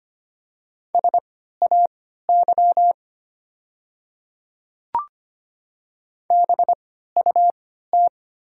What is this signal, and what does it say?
Morse code, heard as a keyed tone, about 25 wpm, spaced as for 14 wpm, 700 Hz. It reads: SAY BUT
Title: 3x Character Spacing - Top 200 Words - 25wpm